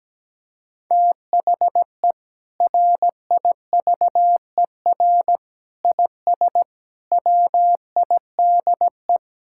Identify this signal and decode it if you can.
Morse code, heard as a keyed tone, about 17 wpm, 700 Hz. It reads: THE RIVER IS WIDE
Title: Sentences from Top 1000 Words - Review Part 6 - 17wpm